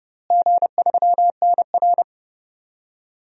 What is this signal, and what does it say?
Morse code, heard as a keyed tone, about 30 wpm, 700 Hz. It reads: G3NR
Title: DXCC-100 Call Signs - 4 Characters - Repeated 3x - 30wpm